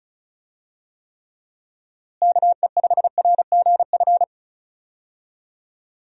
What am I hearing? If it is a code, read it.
Morse code, heard as a keyed tone, about 36 wpm, 700 Hz. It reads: KE5RGF
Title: USA Call Signs - 6 Characters - Repeated 2x - 35wpm